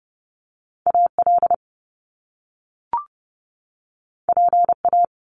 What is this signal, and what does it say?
Morse code, heard as a keyed tone, about 30 wpm, 700 Hz. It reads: AL PA
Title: Top Two Letter Combinations in Words 30wpm